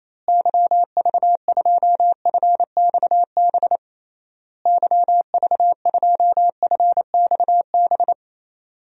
Morse code, heard as a keyed tone, about 28 wpm, 700 Hz. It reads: YV2FXB YV2FXB
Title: DXCC-100 Call Signs - 6 Characters - Repeated 3x - 28wpm